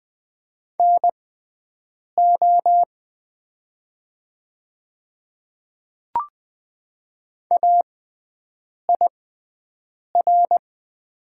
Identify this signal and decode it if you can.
Morse code, heard as a keyed tone, about 20 wpm, spaced as for 7 wpm, 700 Hz. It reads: NO AIR
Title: 6x Character Spacing - Top 200 Words - 20wpm